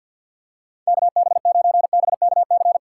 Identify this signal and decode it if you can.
Morse code, heard as a keyed tone, about 48 wpm, 700 Hz. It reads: KB9BKC
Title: USA Call Signs - 6 Characters - 4x Character Spacing - 50wpm